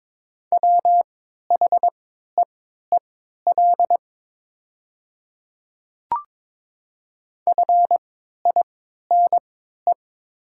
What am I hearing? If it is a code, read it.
Morse code, heard as a keyed tone, about 22 wpm, spaced as for 12 wpm, 700 Hz. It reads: WHEEL FINE
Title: 3x Character Spacing - Top 500 Words - 22wpm